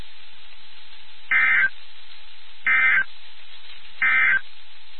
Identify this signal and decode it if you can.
Emergency Alert System (SAME) tones, heard as harsh buzzing bursts. It is heard from 1.2 to 4.5 s.
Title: NOAA Weather Radio WXK79